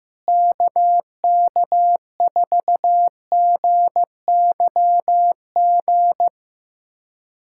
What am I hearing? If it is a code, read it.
Morse code, heard as a keyed tone, about 15 wpm, 700 Hz. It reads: KK4GYG
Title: USA Callsign DE Callsign 15wpm